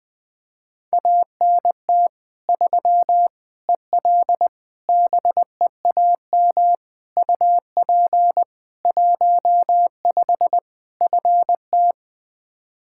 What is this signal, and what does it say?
Morse code, heard as a keyed tone, about 20 wpm, 700 Hz. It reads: ANT 3 EL BEAM UP 15 FT
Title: Practice QSO - Exchange 2 - Info - 20wpm